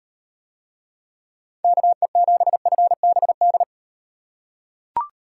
Morse code, heard as a keyed tone, about 38 wpm, 700 Hz. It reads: KE7FBD